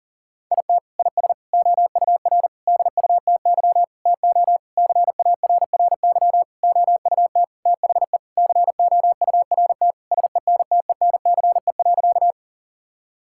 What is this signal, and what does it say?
Morse code, heard as a keyed tone, about 40 wpm, 700 Hz. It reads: IT IS OUR DUTY TO CARRY OUT THE COURT SENTENCE.
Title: Sentences from Top 1600 Words 40wpm